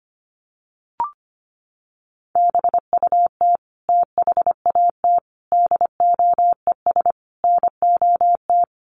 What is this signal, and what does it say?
Morse code, heard as a keyed tone, about 25 wpm, 700 Hz. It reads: BUT THAT DOES NOT
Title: Sets of 5 Words 25wpm